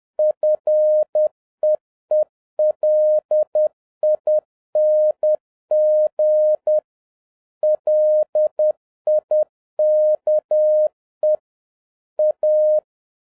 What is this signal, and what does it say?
Morse code, heard as a keyed tone, about 10 wpm, 610 Hz. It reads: FEELING LIKE A